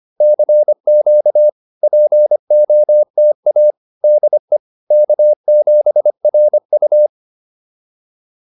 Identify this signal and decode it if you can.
Morse code, heard as a keyed tone, about 25 wpm, 590 Hz. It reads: CQ POTA DE K7RU